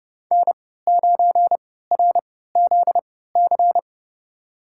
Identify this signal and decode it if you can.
Morse code, heard as a keyed tone, about 31 wpm, spaced as for 17 wpm, 700 Hz. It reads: N9RZC
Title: USA Call Signs - 5 Characters - 3x Character Spacing - 30wpm